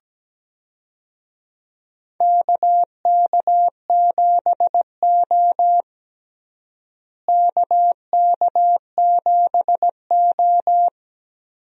Morse code, heard as a keyed tone, about 17 wpm, 700 Hz. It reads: KK7O KK7O
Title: USA Call Signs - 4 Characters - Repeated 3x - 17wpm